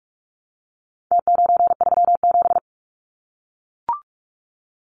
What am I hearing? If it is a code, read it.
Morse code, heard as a keyed tone, about 44 wpm, 700 Hz. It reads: T937
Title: Numbers - Contest Style - 4-Digit 45wpm